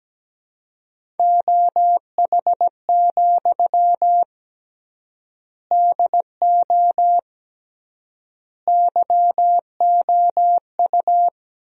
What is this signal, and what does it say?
Morse code, heard as a keyed tone, about 17 wpm, 700 Hz. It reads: OH, DO YOU
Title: 3x Word-Spacing - Sentences from Top 500 Words - 17wpm